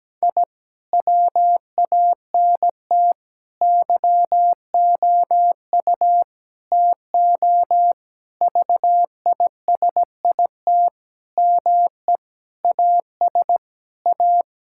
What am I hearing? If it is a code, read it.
Morse code, heard as a keyed tone, about 17 wpm, 700 Hz. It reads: I WANT YOU TO VISIT ME AS A
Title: Sentences from Top 1000 Words - Review Part 6 - 17wpm